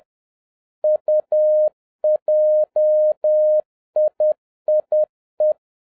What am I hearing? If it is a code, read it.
Morse code, heard as a keyed tone, about 10 wpm, 610 Hz. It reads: UJIIE